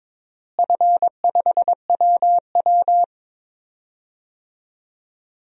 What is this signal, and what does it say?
Morse code, heard as a keyed tone, about 21 wpm, 700 Hz. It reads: F5WW
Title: DXCC-100 Call Signs - 4 Characters - Repeated 3x - 22wpm